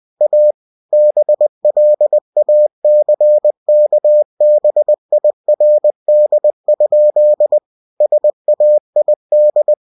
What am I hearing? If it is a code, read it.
Morse code, heard as a keyed tone, about 20 wpm, 600 Hz. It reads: A BLACKBIRD? SAID
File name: cw-books / winnie-the-pooh / winnie-20wpm-page-0069